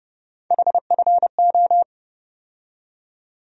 Morse code, heard as a keyed tone, about 31 wpm, 700 Hz. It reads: HFO
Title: USA Call Sign Postfixes - 3 Letters 30wpm